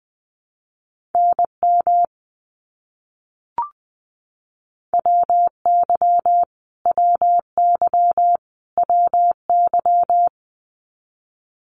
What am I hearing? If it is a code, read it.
Morse code, heard as a keyed tone, about 20 wpm, 700 Hz. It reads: NM WY WY WY
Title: US States Repeated 3x 20wpm